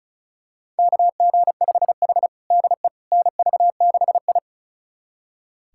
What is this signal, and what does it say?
Morse code, heard as a keyed tone, about 36 wpm, 700 Hz. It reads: KG5H DE NV6I
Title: USA Callsign DE Callsign - 4 Characters - 35wpm